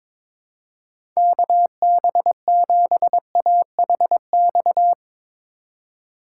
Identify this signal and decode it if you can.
Morse code, heard as a keyed tone, about 22 wpm, 700 Hz. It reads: KB7AHX
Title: USA Call Signs - 8x Character Spacing - 22wpm